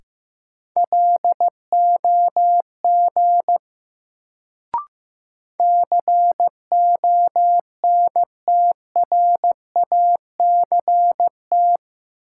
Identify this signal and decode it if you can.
Morse code, heard as a keyed tone, about 15 wpm, 700 Hz. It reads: LOG CONTRACT
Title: Sentences from Top 1400 Words - New Word Drill Alt - 15wpm